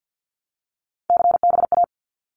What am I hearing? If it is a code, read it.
Morse code, heard as a keyed tone, about 50 wpm, 700 Hz. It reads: CBA
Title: Abbreviations Repeated 3x - 50wpm